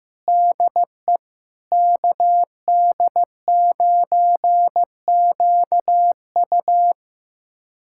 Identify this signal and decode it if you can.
Morse code, heard as a keyed tone, about 16 wpm, 700 Hz. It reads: DE KD9QU
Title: USA Callsign DE Callsign 15wpm